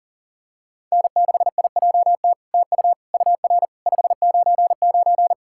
Morse code, heard as a keyed tone, about 40 wpm, 700 Hz. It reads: N6IJT TU UR 599
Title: SOTA QSO Practice 40wpm